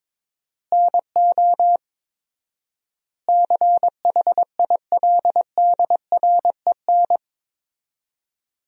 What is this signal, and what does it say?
Morse code, heard as a keyed tone, about 21 wpm, 700 Hz. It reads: NO CHILDREN
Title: 4x Word-Spacing - Sentences from Top 400 Words - 22wpm